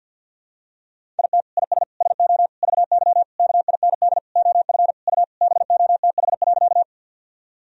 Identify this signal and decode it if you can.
Morse code, heard as a keyed tone, about 51 wpm, 700 Hz. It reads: IT IS SO VY KIND OF U BOTH.
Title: Sentences from Top 300 Words and Abbreviations 50wpm